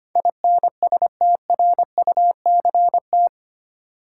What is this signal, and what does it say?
Morse code, heard as a keyed tone, about 25 wpm, 700 Hz. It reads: INSTRUCT
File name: practice / 25wpm / 8 Letter Words - First 3 Letters Given - 25wpm